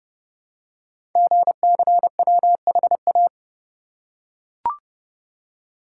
Morse code, heard as a keyed tone, about 30 wpm, 700 Hz. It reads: GCWHA
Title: Code Groups - 5 Letter:Numbers - 30wpm